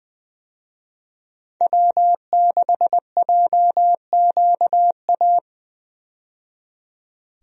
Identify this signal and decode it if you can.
Morse code, heard as a keyed tone, about 20 wpm, 700 Hz. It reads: W6JQA
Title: USA Call Signs - 5 Characters - 20wpm